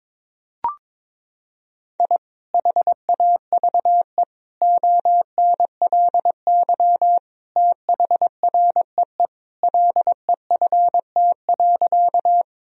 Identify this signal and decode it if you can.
Morse code, heard as a keyed tone, about 22 wpm, 700 Hz. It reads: I HAVE ONLY THREE LEFT.
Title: Sentences from Top 1000 Words - Review Part 1 - 22wpm